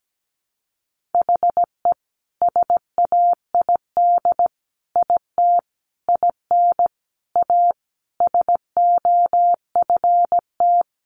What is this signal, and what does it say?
Morse code, heard as a keyed tone, about 17 wpm, 700 Hz. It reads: HE SAID IT IN A SOFT
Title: Sentences from Top 1500 Words 17wpm